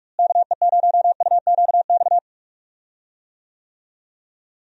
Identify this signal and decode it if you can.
Morse code, heard as a keyed tone, about 46 wpm, 700 Hz. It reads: KE0UQX
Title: USA Call Signs - 6 Characters - Repeated 3x - 45wpm